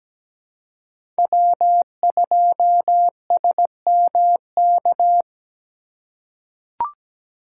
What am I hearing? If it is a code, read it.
Morse code, heard as a keyed tone, about 17 wpm, 700 Hz. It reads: W2SMK
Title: USA Call Signs - 4x Character Spacing - 17wpm